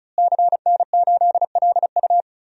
Morse code, heard as a keyed tone, about 36 wpm, 700 Hz. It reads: CN8LU
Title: DXCC-100 Call Signs - 5 Characters - Repeated 2x - 35wpm